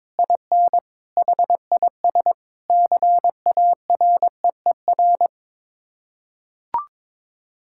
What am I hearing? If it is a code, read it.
Morse code, heard as a keyed tone, about 21 wpm, 700 Hz. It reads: IN HIS CAREER
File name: practice / 22wpm / Sets of 7 Words - 22wpm